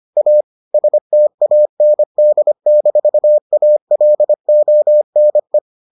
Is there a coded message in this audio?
Yes: Morse code from 0.2 to 5.6 s, heard as a keyed tone, about 25 wpm, 600 Hz. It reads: A STAND-ALONE